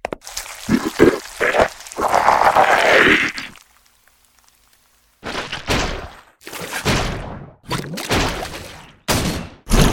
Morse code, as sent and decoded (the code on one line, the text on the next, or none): none